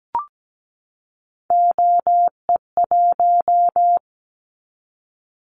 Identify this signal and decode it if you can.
Morse code, heard as a keyed tone, about 17 wpm, 700 Hz. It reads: OE1
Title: Callsign Prefixes for Top 50 DXCC - 17wpm